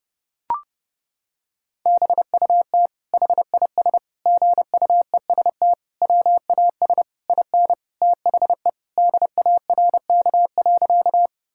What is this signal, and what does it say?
Morse code, heard as a keyed tone, about 30 wpm, 700 Hz. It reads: BUT HIS GUEST WAS IN THE DARK.
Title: Sentences from Top 1600 Words 30wpm